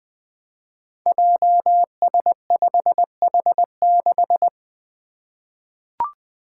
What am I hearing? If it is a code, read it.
Morse code, heard as a keyed tone, about 20 wpm, 700 Hz. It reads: JS5H6
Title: Five Challenging Letter-Number Combinations 20wpm